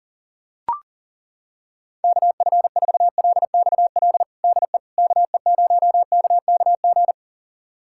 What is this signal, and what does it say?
Morse code, heard as a keyed tone, about 40 wpm, 700 Hz. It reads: KF4LXL DE KE0KKG
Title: USA Callsign DE Callsign - 6 Characters - 40wpm